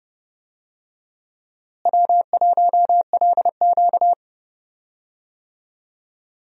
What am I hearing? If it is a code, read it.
Morse code, heard as a keyed tone, about 30 wpm, 700 Hz. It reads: W1LQ
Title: USA Call Signs - 4 Characters - Repeated 3x - 30wpm